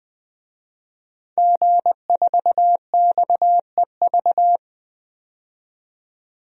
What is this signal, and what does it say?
Morse code, heard as a keyed tone, about 21 wpm, 700 Hz. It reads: G4XEV
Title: DXCC-100 Call Signs - 5 Characters - 20wpm